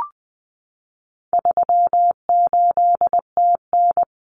Morse code, heard as a keyed tone, about 20 wpm, 700 Hz. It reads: 38TN
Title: Common Cut Numbers - 4-Digit 20wpm